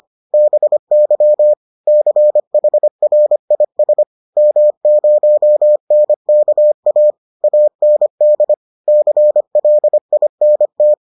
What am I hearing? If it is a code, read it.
Morse code, heard as a keyed tone, about 25 wpm, 600 Hz. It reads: BY CHRIS M0NKA AND CLINT